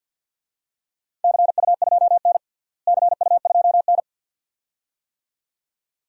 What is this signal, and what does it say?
Morse code, heard as a keyed tone, about 51 wpm, 700 Hz. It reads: CU2N CU2N
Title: DXCC-100 Call Signs - 4 Characters - Repeated 2x - 50wpm